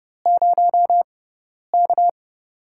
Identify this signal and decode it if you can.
Morse code, heard as a keyed tone, about 31 wpm, spaced as for 10 wpm, 700 Hz. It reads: 0K
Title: USA Call Signs - 4 Characters - 6x Character Spacing - 30wpm